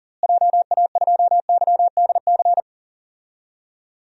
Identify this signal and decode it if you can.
Morse code, heard as a keyed tone, about 40 wpm, 700 Hz. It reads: JA2YDC
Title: DXCC-100 Call Signs - 6 Characters - Repeated 3x - 40wpm